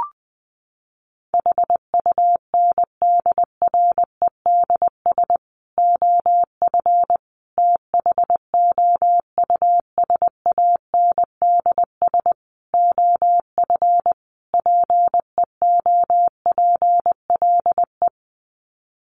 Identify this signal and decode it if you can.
Morse code, heard as a keyed tone, about 20 wpm, 700 Hz. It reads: HUNDREDS OF THOUSANDS OF PEOPLE
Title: Sets of 5 Words Encore2 20wpm